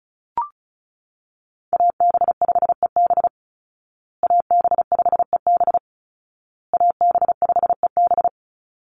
Morse code, heard as a keyed tone, about 34 wpm, 700 Hz. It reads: AB5EB AB5EB AB5EB
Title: Call Signs Repeated 2x - 35wpm